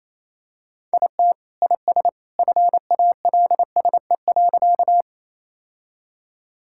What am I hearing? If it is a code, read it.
Morse code, heard as a keyed tone, about 28 wpm, 700 Hz. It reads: IT IS FALSE.
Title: Sentences from Top 1600 Words 28wpm